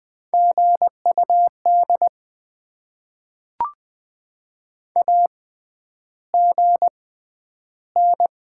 Morse code, heard as a keyed tone, about 20 wpm, spaced as for 10 wpm, 690 Hz. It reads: GUD AGN
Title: Top 20 QSO Elements - 6x Character Spacing - 20wpm